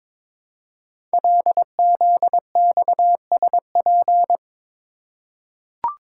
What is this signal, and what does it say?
Morse code, heard as a keyed tone, about 22 wpm, 700 Hz. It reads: LZXSP
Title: Code Groups - 5 Letters - 22wpm